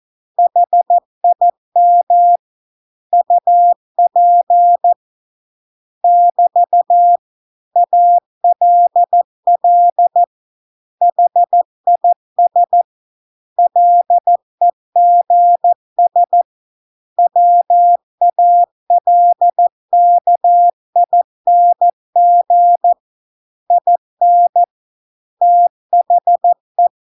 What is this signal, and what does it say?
Morse code, heard as a keyed tone, about 14 wpm, 700 Hz. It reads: HIM UP <BT> ALL HIS LEGS WALKING IN THE